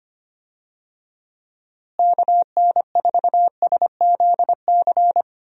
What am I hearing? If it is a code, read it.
Morse code, heard as a keyed tone, about 25 wpm, 700 Hz. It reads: KN4SZC
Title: USA Call Signs - 6 Characters - Repeated 2x - 25wpm